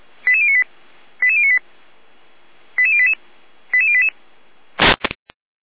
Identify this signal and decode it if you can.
Five-tone selective call (ZVEI), heard as quick runs of five tones, tones of about 70 ms. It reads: 88098 88098 88088 88088